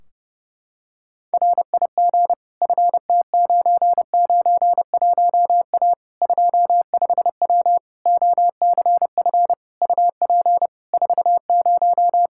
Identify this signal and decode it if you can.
Morse code, heard as a keyed tone, about 30 wpm, 700 Hz. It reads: RIG FT991A 25W OCF UP 40
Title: Practice QSO - Exchange 2 - Info - 30wpm